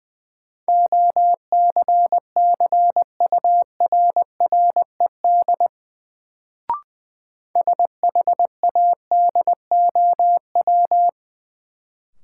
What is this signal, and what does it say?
Morse code, heard as a keyed tone, about 20 wpm, 700 Hz. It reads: OCCURRED SHADOW
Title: Sets of 5 Words Encore - New Word Drill - 20wpm